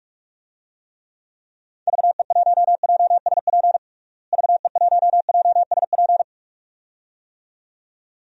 Morse code, heard as a keyed tone, about 46 wpm, 700 Hz. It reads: VE1JSP VE1JSP
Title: DXCC-100 Call Signs - 6 Characters - Repeated 2x - 45wpm